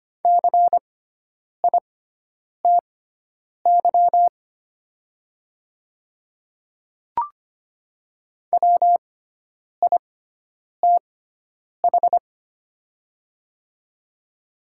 Morse code, heard as a keyed tone, about 25 wpm, spaced as for 9 wpm, 700 Hz. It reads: CITY WITH
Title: 6x Character Spacing - Top 300 Words - 25wpm